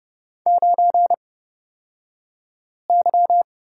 Morse code, heard as a keyed tone, about 30 wpm, spaced as for 5 wpm, 700 Hz. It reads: 9Y
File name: Lesson 38d - Letter q - Two Characters 30wpm